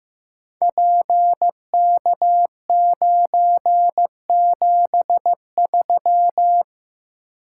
Morse code, heard as a keyed tone, about 15 wpm, 700 Hz. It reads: PK973
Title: Code Groups - 5 Letter:Numbers - 15wpm